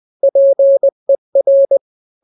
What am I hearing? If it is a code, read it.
Morse code, heard as a keyed tone, about 20 wpm, 550 Hz. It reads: PER
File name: n-grams / 3-grams 20 18 WPM